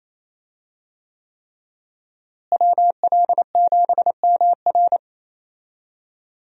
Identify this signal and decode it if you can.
Morse code, heard as a keyed tone, about 28 wpm, 700 Hz. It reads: WL7MR
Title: USA Call Signs - 5 Characters - Repeated 3x - 28wpm